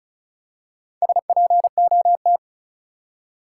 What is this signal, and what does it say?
Morse code, heard as a keyed tone, about 36 wpm, 700 Hz. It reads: SPOT
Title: Four Letter Words - 35wpm